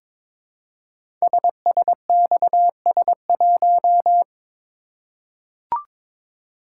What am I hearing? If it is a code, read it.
Morse code, heard as a keyed tone, about 22 wpm, 700 Hz. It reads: SSXS1
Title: Code Groups - 5 Letter:Numbers - 22wpm